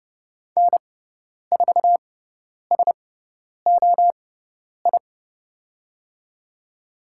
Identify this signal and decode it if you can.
Morse code, heard as a keyed tone, about 30 wpm, spaced as for 10 wpm, 700 Hz. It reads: N4SOI
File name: Lesson 18g - Letter m - Call Signs 30wpm